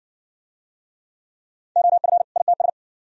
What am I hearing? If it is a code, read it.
Morse code, heard as a keyed tone, about 56 wpm, 700 Hz. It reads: OF ITS